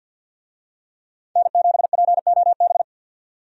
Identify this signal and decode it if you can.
Morse code, heard as a keyed tone, about 53 wpm, 700 Hz. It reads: N7POB
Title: USA Call Signs - 5 Characters - 50wpm